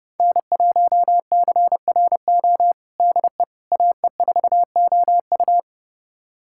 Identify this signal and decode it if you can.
Morse code, heard as a keyed tone, about 30 wpm, 700 Hz. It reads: N1CRO DE AE4OU
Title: USA Callsign DE Callsign - 5 Characters - 30wpm